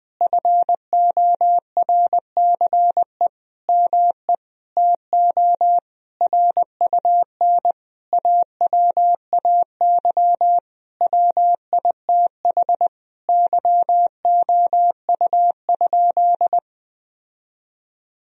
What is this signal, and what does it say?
Morse code, heard as a keyed tone, about 21 wpm, 700 Hz. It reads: FORCE ME TO RUN AWAY WITH YOU?
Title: Sentences from Top 1000 Words - Review Part 3 - 20wpm